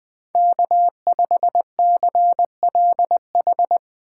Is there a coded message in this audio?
Yes: Morse code from 0.3 to 3.8 s, heard as a keyed tone, about 20 wpm, 700 Hz. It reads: K5CLH